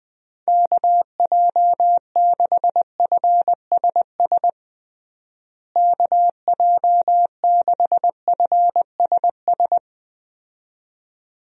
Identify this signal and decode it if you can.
Morse code, heard as a keyed tone, about 20 wpm, 700 Hz. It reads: KJ6FSS KJ6FSS
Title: USA Call Signs - 6 Characters - Repeated 2x - 20wpm